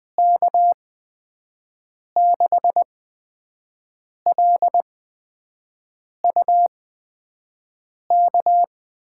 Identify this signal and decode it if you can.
Morse code, heard as a keyed tone, about 20 wpm, spaced as for 5 wpm, 700 Hz. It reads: K6LUK
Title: USA Call Signs - 6 Characters - 8x Character Spacing - 20wpm